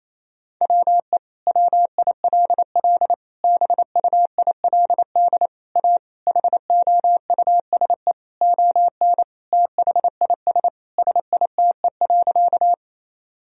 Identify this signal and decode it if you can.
Morse code, heard as a keyed tone, about 27 wpm, 700 Hz. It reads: WE WILL BUILD A HOUSE ON THIS SITE.
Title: Sentences from Top 1000 Words - Review Part 4 - 28wpm